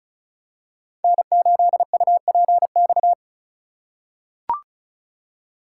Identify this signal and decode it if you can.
Morse code, heard as a keyed tone, about 34 wpm, 700 Hz. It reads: N8UPX
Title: USA Call Signs - 4x Character Spacing - 35wpm